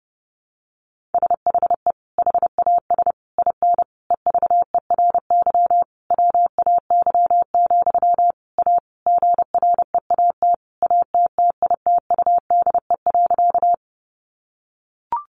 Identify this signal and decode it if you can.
Morse code, heard as a keyed tone, about 30 wpm, 700 Hz. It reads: SHE HAS IN EVERY WAY, A GREAT ATTITUDE.
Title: Sentences from Top 1300 Words 30wpm